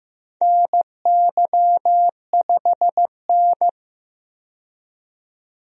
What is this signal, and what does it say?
Morse code, heard as a keyed tone, about 16 wpm, 700 Hz. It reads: NY5N
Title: USA Call Signs - 4 Characters - 15wpm